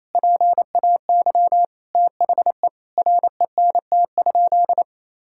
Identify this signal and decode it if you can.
Morse code, heard as a keyed tone, about 28 wpm, 700 Hz. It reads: PAY THE RENT?
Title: Sentences from Top 1000 Words - Review Part 5 - 28wpm